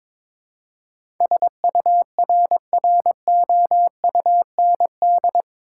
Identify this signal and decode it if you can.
Morse code, heard as a keyed tone, about 22 wpm, 700 Hz. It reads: SURROUND
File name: practice / 22wpm / Eight Letter Words - 22wpm